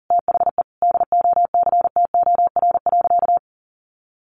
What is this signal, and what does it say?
Morse code, heard as a keyed tone, about 40 wpm, 700 Hz. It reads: THE DOCTOR.
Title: Sentences from Top 1500 Words 40wpm